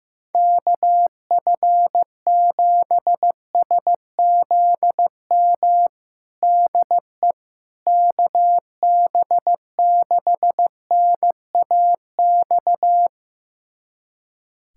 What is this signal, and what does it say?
Morse code, heard as a keyed tone, about 15 wpm, 700 Hz. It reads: KF7SZM DE KB6NAX
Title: USA Callsign DE Callsign - 6 Characters - 15wpm